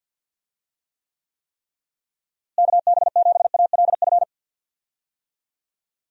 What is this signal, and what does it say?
Morse code, heard as a keyed tone, about 52 wpm, 700 Hz. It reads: KB7ALF
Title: USA Call Signs - 6 Characters - Repeated 3x - 50wpm